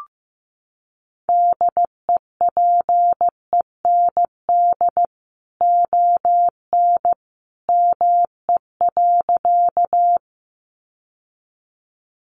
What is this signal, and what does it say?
Morse code, heard as a keyed tone, about 15 wpm, 700 Hz. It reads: DEPEND ON ME.
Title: Sentences from Top 1300 Words 15wpm